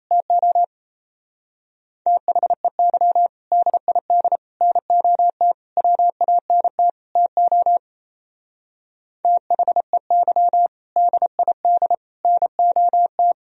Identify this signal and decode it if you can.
Morse code, heard as a keyed tone, about 31 wpm, 700 Hz. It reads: TO THEY DID NOT WANT TO THEY DID NOT
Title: Speed-Racing - Sets of 5 Words 28wpm